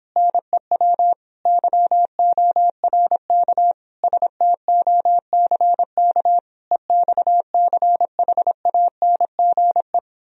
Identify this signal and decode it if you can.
Morse code, heard as a keyed tone, about 26 wpm, 700 Hz. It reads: NEW YORK STOCK EXCHANGE